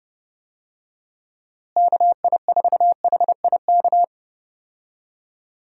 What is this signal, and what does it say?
Morse code, heard as a keyed tone, about 30 wpm, 700 Hz. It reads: KI4HIK
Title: USA Call Signs - 6 Characters - Repeated 3x - 30wpm